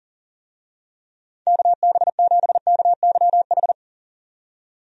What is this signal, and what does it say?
Morse code, heard as a keyed tone, about 37 wpm, 690 Hz. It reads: KB7KYH
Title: USA Call Signs - 6 Characters - 40wpm